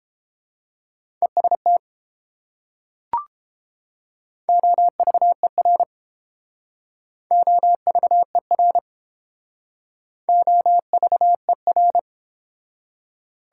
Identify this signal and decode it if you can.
Morse code, heard as a keyed tone, about 32 wpm, 700 Hz. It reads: EST OVER OVER OVER